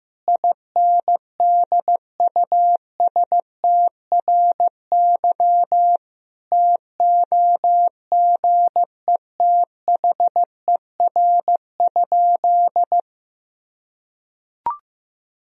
Morse code, heard as a keyed tone, about 16 wpm, 700 Hz. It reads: INDUSTRY TOGETHER?
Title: Sentences from Top 1000 Words - Review Part 4 - 15wpm